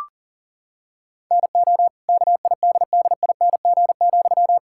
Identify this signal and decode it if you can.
Morse code, heard as a keyed tone, about 40 wpm, 700 Hz. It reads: NO KIDDING,
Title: Pangram Challenge v2 - 40wpm